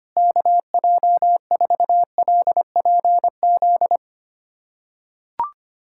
Morse code, heard as a keyed tone, about 25 wpm, 700 Hz. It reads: KJ4LPZ
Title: USA Callsign DE Callsign - 6 Characters - 25wpm